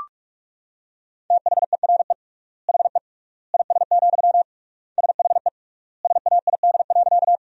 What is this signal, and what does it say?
Morse code, heard as a keyed tone, about 43 wpm, 690 Hz. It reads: THERE HE IS, SHE SAID.
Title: 3x Word-Spacing - Sentences from Top 100 Words - 45wpm